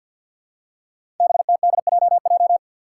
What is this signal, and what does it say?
Morse code, heard as a keyed tone, about 52 wpm, 700 Hz. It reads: BTDJJ
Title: Five Challenging Letter-Number Combinations 50wpm